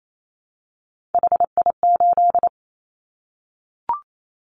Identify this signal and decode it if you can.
Morse code, heard as a keyed tone, about 28 wpm, 700 Hz. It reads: HI8
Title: Callsign Prefixes for Top 100 DXCC - 28wpm